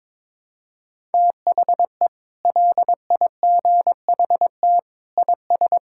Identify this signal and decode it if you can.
Morse code, heard as a keyed tone, about 22 wpm, 700 Hz. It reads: THE LIGHT IS
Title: Sentences from Top 1000 Words - Review Part 6 - 22wpm